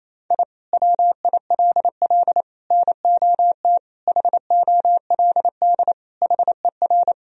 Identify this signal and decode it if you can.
Morse code, heard as a keyed tone, about 28 wpm, 700 Hz. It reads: I WILL NOT HOLD HER
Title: Sentences from Top 1000 Words - Review Part 2 - 28wpm